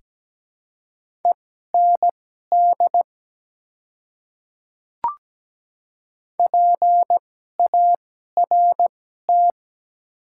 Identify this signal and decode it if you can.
Morse code, heard as a keyed tone, about 17 wpm, spaced as for 12 wpm, 700 Hz. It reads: END PART
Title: 2x Character Spacing - Top 200 Words - 17wpm